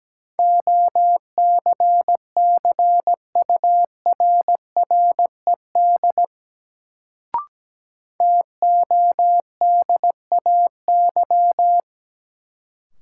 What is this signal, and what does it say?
Morse code, heard as a keyed tone, about 17 wpm, 700 Hz. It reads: OCCURRED TODAY
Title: Sets of 5 Words Encore - New Word Drill - 17wpm